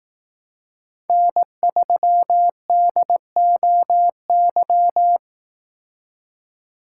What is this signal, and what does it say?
Morse code, heard as a keyed tone, about 18 wpm, 700 Hz. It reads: N3DOY
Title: Speed Racing - USA Call Signs 15wpm